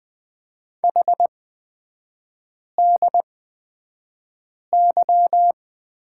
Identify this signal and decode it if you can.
Morse code, heard as a keyed tone, about 20 wpm, spaced as for 5 wpm, 700 Hz. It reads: HDY
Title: 224 Lesson 36e - BT - Three Characters 20wpm